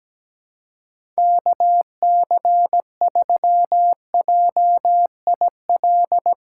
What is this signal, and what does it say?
Morse code, heard as a keyed tone, about 17 wpm, 700 Hz. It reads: KC3JIL
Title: USA Call Signs - 6 Characters - 2x Character Spacing - 17wpm